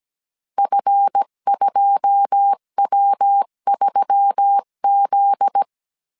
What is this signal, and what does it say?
Morse code, heard as a keyed tone, about 17 wpm, 800 Hz. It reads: F2W3Z
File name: CW / DET / 1 9 1500C S B 16wpm 2856